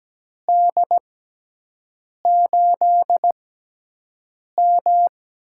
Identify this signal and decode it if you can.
Morse code, heard as a keyed tone, about 17 wpm, spaced as for 6 wpm, 700 Hz. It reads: D8M
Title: USA Call Signs - 6 Characters - 6x Character Spacing - 17wpm